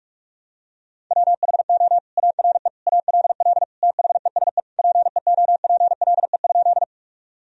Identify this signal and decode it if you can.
Morse code, heard as a keyed tone, about 42 wpm, 700 Hz. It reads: WHO ARE ALL THESE PEOPLE?
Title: Sentences from Top 1000 Words - Review Part 1 - 45wpm